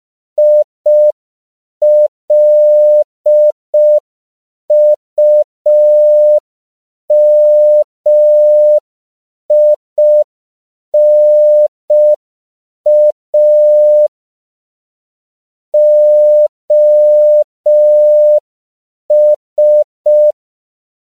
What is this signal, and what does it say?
Morse code, heard as a keyed tone, about 5 wpm, 600 Hz. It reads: ILUMINA OS
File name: CW / 5PPM